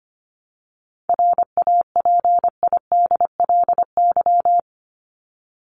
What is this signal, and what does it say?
Morse code, heard as a keyed tone, about 25 wpm, 700 Hz. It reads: RAPIDLY
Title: LY Suffix - Top 100 Words with Base Words 25wpm